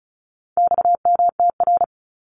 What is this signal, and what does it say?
Morse code, heard as a keyed tone, about 34 wpm, 700 Hz. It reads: XMTR
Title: Abbreviations 35wpm